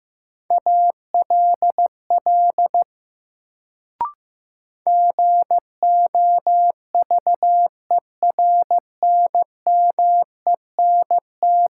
Morse code, heard as a keyed tone, about 15 wpm, 700 Hz. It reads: ALL GOVERNMENT
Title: Sets of 2 Words - New Word Drill - 15wpm